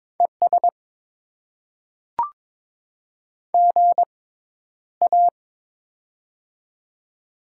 Morse code, heard as a keyed tone, about 22 wpm, spaced as for 11 wpm, 700 Hz. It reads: ES GA
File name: Top 20 QSO Elements - 6x Character Spacing - 22wpm